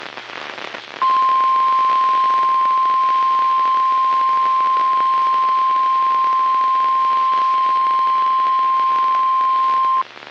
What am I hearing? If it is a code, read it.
Emergency Alert System (SAME) tones (the 1050 Hz attention tone). It is heard from 1.0 to 10.0 s.